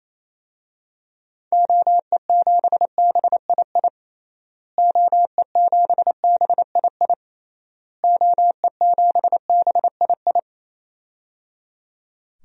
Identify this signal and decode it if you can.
Morse code, heard as a keyed tone, about 28 wpm, 700 Hz. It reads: OE7BII OE7BII OE7BII
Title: DXCC-100 Call Signs - 6 Characters - Repeated 3x - 28wpm